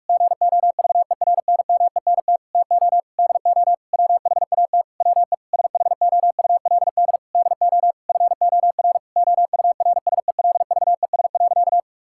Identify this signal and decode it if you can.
Morse code, heard as a keyed tone, about 41 wpm, 700 Hz. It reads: GOVERNMENT TO DO WHAT WE SHOULD DO FOR OURSELVES.
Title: Sentences from Top 1600 Words 45wpm